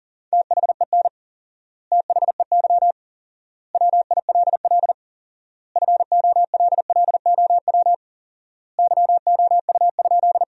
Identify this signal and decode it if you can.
Morse code, heard as a keyed tone, about 40 wpm, 700 Hz. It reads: THEN THEY WILL FOLLOW YOU?
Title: 4x Word-Spacing - Sentences from Top 200 Words - 40wpm